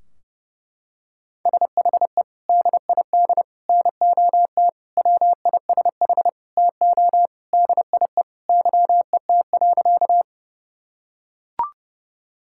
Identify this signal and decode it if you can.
Morse code, heard as a keyed tone, about 30 wpm, 700 Hz. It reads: SHE DID NOT WISH TO DIE YET.
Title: Sentences from Top 1600 Words 30wpm